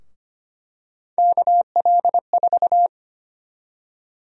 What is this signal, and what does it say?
Morse code, heard as a keyed tone, about 25 wpm, 700 Hz. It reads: KL4
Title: USA Call Sign Prefixes - 25wpm